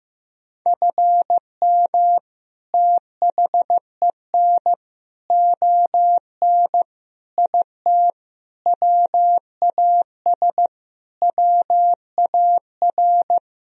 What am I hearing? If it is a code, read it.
Morse code, heard as a keyed tone, about 15 wpm, 700 Hz. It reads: FM THEN ON IT WAS WAR
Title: Sentences from Top 400 Words and Abbreviations 15wpm